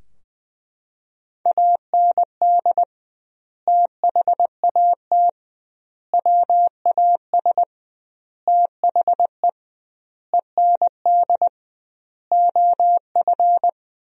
Morse code, heard as a keyed tone, about 20 wpm, 700 Hz. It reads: AND THAT WAS THE END OF
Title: 2x Word-Spacing - Sentences from Top 200 Words - 20wpm